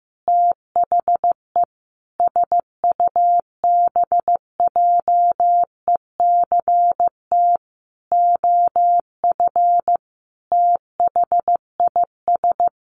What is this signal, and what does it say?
Morse code, heard as a keyed tone, about 15 wpm, 700 Hz. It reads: THE SUBJECT OF THIS
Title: Sentences from Top 1100 Words 15wpm